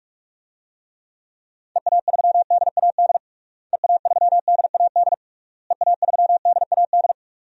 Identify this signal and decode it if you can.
Morse code, heard as a keyed tone, about 46 wpm, 700 Hz. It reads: EA3DAD EA3DAD EA3DAD
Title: DXCC-100 Call Signs - 6 Characters - Repeated 3x - 45wpm